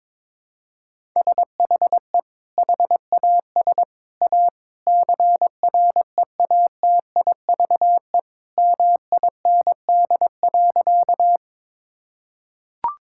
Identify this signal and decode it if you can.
Morse code, heard as a keyed tone, about 22 wpm, 700 Hz. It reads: SHE HAS A CREATIVE MIND.
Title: Sentences from Top 1000 Words - Review Part 5 - 22wpm